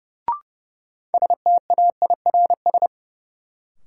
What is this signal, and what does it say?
Morse code, heard as a keyed tone, about 30 wpm, 700 Hz. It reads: STAIRS
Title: Sets of 5 Words Encore - New Word Drill - 30wpm